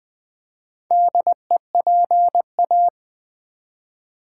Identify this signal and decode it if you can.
Morse code, heard as a keyed tone, about 20 wpm, 690 Hz. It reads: DEPA
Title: Build A Word - 2 Letters At A Time - 6-Letter Words - 20wpm